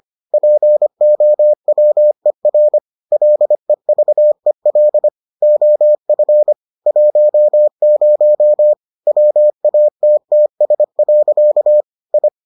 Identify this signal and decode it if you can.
Morse code, heard as a keyed tone, about 25 wpm, 600 Hz. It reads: POWER LEVEL OF 10 WATTS. I